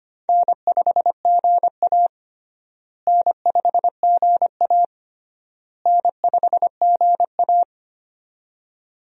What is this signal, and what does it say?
Morse code, heard as a keyed tone, about 25 wpm, 700 Hz. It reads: N5GA N5GA N5GA
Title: USA Call Signs - 4 Characters - Repeated 3x - 25wpm